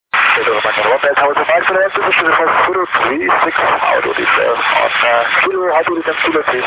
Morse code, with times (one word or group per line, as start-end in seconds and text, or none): none